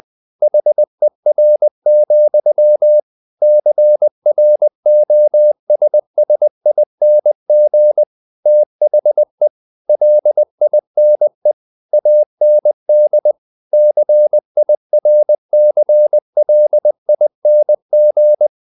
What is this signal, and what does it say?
Morse code, heard as a keyed tone, about 20 wpm, 600 Hz. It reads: HER, CROSSING THE LINE AND CIRCLING
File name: the old man and the sea / the old man and the sea part 1 20wpm